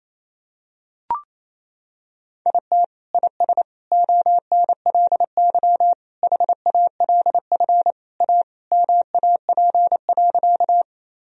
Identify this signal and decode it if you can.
Morse code, heard as a keyed tone, about 28 wpm, 700 Hz. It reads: IT IS ONLY HALF A MAP.
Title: Sentences from Top 1000 Words - Review Part 2 - 28wpm